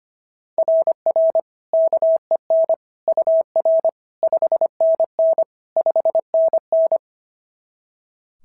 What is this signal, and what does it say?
Morse code, heard as a keyed tone, about 25 wpm, 690 Hz. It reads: RR KEN UR 5NN 5NN
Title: SOTA QSO Practice 25wpm